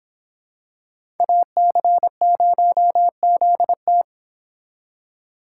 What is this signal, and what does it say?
Morse code, heard as a keyed tone, about 25 wpm, 700 Hz. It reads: AC0ZT